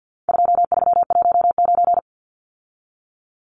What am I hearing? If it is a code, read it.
Morse code, heard as a keyed tone, about 50 wpm, 700 Hz. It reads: 2319
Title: Numbers - 4-Digit 50wpm